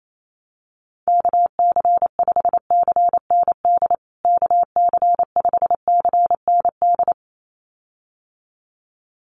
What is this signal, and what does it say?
Morse code, heard as a keyed tone, about 28 wpm, 700 Hz. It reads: KC5CND KC5CND
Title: USA Call Signs Repeated 2x 28wpm